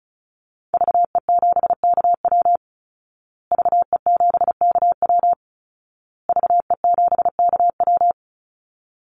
Morse code, heard as a keyed tone, about 34 wpm, 700 Hz. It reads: VE7KW VE7KW VE7KW
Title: Call Signs Repeated 2x - 35wpm